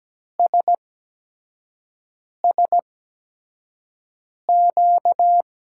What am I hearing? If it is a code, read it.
Morse code, heard as a keyed tone, about 17 wpm, spaced as for 5 wpm, 700 Hz. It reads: SSQ
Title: USA Call Signs - 6 Characters - 8x Character Spacing - 17wpm